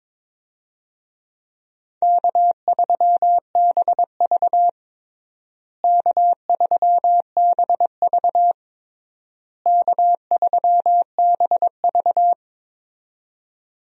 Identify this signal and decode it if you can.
Morse code, heard as a keyed tone, about 22 wpm, 700 Hz. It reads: K3BV K3BV K3BV
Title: USA Call Signs - 4 Characters - Repeated 3x - 22wpm